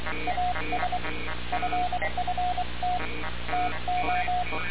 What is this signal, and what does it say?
Morse code, heard as a keyed tone, about 23 wpm, 720 Hz, with noise. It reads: AS FFT TO
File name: radiocap / REC z